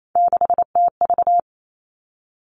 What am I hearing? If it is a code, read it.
Morse code, heard as a keyed tone, about 28 wpm, 690 Hz. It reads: 6TV
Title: Three Challenging Letter-Number Combinations 28wpm